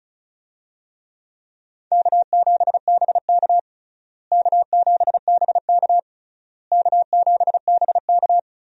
Morse code, heard as a keyed tone, about 34 wpm, 700 Hz. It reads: K7BK K7BK K7BK
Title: USA Call Signs - 4 Characters - Repeated 3x - 35wpm